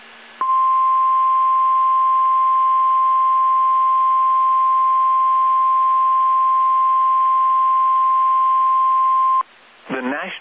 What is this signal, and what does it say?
Emergency Alert System (SAME) tones (the 1050 Hz attention tone). It is heard from 0.4 to 9.5 s.